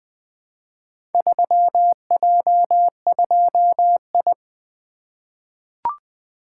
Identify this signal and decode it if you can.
Morse code, heard as a keyed tone, about 20 wpm, 700 Hz. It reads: 3J2I
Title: Four Letter-Number Combinations - 20wpm